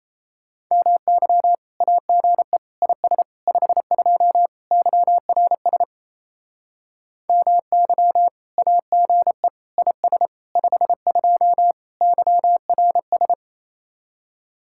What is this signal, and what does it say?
Morse code, heard as a keyed tone, about 35 wpm, 700 Hz. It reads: MY AGE IS 52 YRS MY AGE IS 52 YRS
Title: Speed-Racing Common QSO Phrases - 28wpm